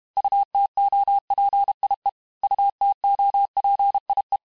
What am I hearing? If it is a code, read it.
Morse code, heard as a keyed tone, about 32 wpm, 800 Hz. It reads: UTOPIE UTOPIE
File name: cours-du-mercredi / audios / cours-868-teste-MOTS-vitesse-30mpm